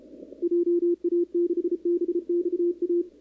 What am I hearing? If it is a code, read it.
Morse code, heard as a keyed tone, about 31 wpm, 340 Hz, with radio noise. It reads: JA6BXA
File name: contests / 2019-cw-wpx-kt5j-k5pi- -2330 0-11